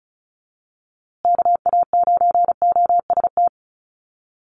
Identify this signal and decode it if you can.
Morse code, heard as a keyed tone, about 34 wpm, 700 Hz. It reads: KA9OST
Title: USA Call Signs Repeated 2x 35wpm